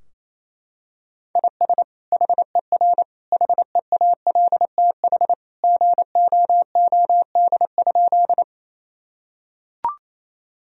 Morse code, heard as a keyed tone, about 28 wpm, 700 Hz. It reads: IS HER HEALTH GOOD?
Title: Sentences from Top 1000 Words - Review Part 3 - 28wpm